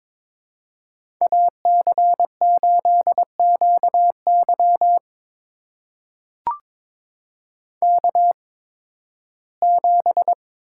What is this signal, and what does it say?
Morse code, heard as a keyed tone, about 23 wpm, spaced as for 15 wpm, 700 Hz. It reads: AC8QY K7
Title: USA Call Signs - 5 Characters - 8x Character Spacing - 22wpm